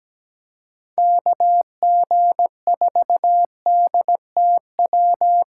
Morse code, heard as a keyed tone, about 17 wpm, 700 Hz. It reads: KG4DTW
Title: USA Call Signs - 3x Character Spacing - 17wpm